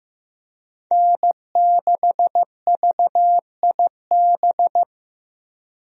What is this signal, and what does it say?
Morse code, heard as a keyed tone, about 15 wpm, 700 Hz. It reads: N6VIB
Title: USA Call Signs - 5 Characters - 15wpm